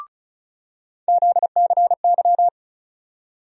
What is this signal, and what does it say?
Morse code, heard as a keyed tone, about 36 wpm, 700 Hz. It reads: ZCY
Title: USA Call Sign Postfixes - 3 Letters 35wpm